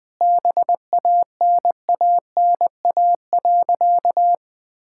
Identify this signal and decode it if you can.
Morse code, heard as a keyed tone, about 20 wpm, 700 Hz. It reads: BANANA.
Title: Sentences with 2-Word Phrases - 20wpm